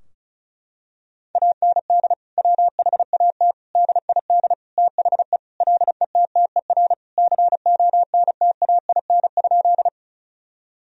Morse code, heard as a keyed tone, about 36 wpm, 700 Hz. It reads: AND WHAT DID THE LETTER CONTAIN?
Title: Sentences from Top 1000 Words - Review Part 2 - 35wpm